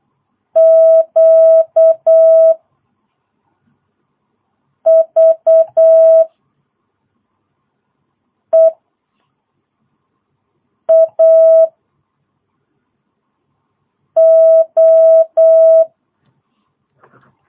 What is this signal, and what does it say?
Morse code, heard as a keyed tone, about 8 wpm, 650 Hz. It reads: Q V E A O
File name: jh8cbh / 2022cw